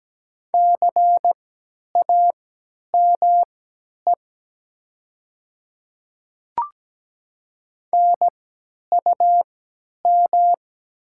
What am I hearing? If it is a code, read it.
Morse code, heard as a keyed tone, about 17 wpm, spaced as for 10 wpm, 700 Hz. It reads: CAME NUM
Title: 3x Character Spacing - Top 200 Words - 17wpm